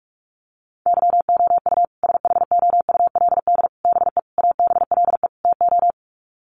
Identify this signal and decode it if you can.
Morse code, heard as a keyed tone, about 44 wpm, 700 Hz. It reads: YOU SHOULD BE ABLE TO
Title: Sets of 5 Words Encore 45wpm